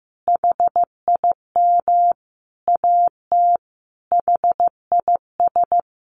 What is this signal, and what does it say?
Morse code, heard as a keyed tone, about 15 wpm, 700 Hz. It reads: HIM AT HIS
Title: Sentences from Top 1500 Words 15wpm